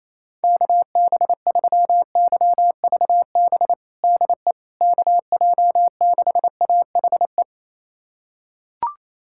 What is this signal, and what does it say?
Morse code, heard as a keyed tone, about 28 wpm, 700 Hz. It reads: KB3YVB DE KJ6AHE
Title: USA Callsign DE Callsign - 6 Characters - 28wpm